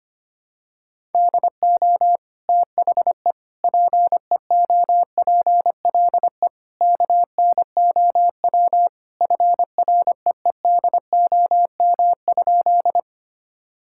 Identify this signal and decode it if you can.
Morse code, heard as a keyed tone, about 25 wpm, 700 Hz. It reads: DO THE PEOPLE KNOW FREEDOM?
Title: Sentences from Top 1000 Words - Review Part 5 - 25wpm